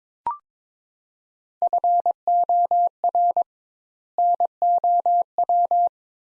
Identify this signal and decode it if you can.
Morse code, heard as a keyed tone, about 22 wpm, 700 Hz. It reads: FOR NOW
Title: Sentences Top 200 Words - 22wpm